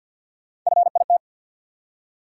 Morse code, heard as a keyed tone, about 53 wpm, 700 Hz. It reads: FIT
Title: Three Letter Words - 50wpm